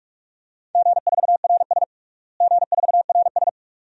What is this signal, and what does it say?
Morse code, heard as a keyed tone, about 46 wpm, 700 Hz. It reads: G4RS G4RS
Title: DXCC-100 Call Signs - 4 Characters - Repeated 2x - 45wpm